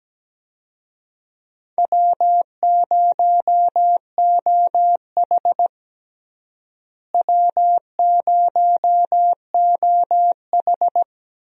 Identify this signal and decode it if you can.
Morse code, heard as a keyed tone, about 17 wpm, 700 Hz. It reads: W0OH W0OH
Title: USA Call Signs - 4 Characters - Repeated 3x - 17wpm